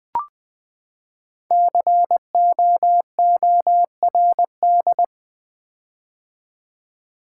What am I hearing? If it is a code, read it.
Morse code, heard as a keyed tone, about 20 wpm, 700 Hz. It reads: COORD
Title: Build A Word - 5 Letters At A Time - 12-Letter Words - 20wpm